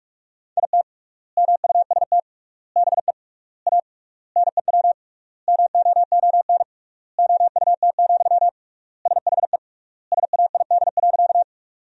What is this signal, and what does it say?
Morse code, heard as a keyed tone, about 42 wpm, 700 Hz. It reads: IT MUST BE A NEW MOON OUT, SHE SAID.
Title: 3x Word-Spacing - Sentences from Top 500 Words - 45wpm